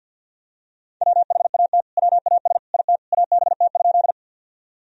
Morse code, heard as a keyed tone, about 52 wpm, 700 Hz. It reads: WHAT WAS IT ABT?
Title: Sentences from Top 100 Words and Abbreviations 50wpm